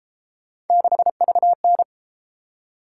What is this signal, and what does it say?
Morse code, heard as a keyed tone, about 31 wpm, 700 Hz. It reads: 6VN